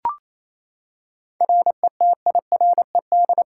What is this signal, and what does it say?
Morse code, heard as a keyed tone, about 27 wpm, 700 Hz. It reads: RETIRED
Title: K1USN SST QSO Practice 28wpm